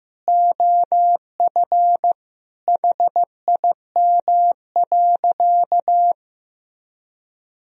Morse code, heard as a keyed tone, about 15 wpm, 700 Hz. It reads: OF HIM.
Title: Sentences from Top 1000 Words - Review Part 4 - 15wpm